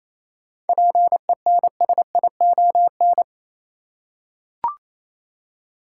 Morse code, heard as a keyed tone, about 27 wpm, 700 Hz. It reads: PENSION
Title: Seven Letter Words - 28wpm